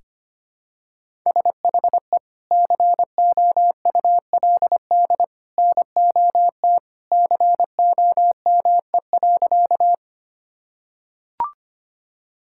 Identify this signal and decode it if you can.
Morse code, heard as a keyed tone, about 25 wpm, 700 Hz. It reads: SHE COULD NOT COME.
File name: practice / 25wpm / Sentences with 2-Word Phrases - 25wpm